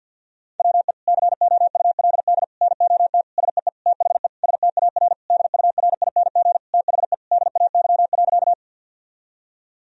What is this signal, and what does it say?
Morse code, heard as a keyed tone, about 50 wpm, 700 Hz. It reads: WE COULD NOT SEE THE STAR DURING THE DAY.
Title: Sentences from Top 1000 Words - Review Part 2 - 50wpm